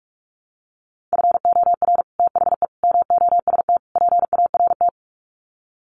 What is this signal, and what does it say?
Morse code, heard as a keyed tone, about 44 wpm, 700 Hz. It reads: FOR THE MOST PART
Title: Sets of 4 Words 45wpm